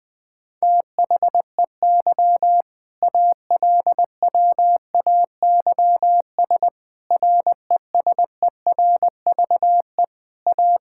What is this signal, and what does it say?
Morse code, heard as a keyed tone, about 20 wpm, 700 Hz. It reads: THEY ALWAYS RESERVE A
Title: Sentences from Top 1000 Words - Review Part 6 - 20wpm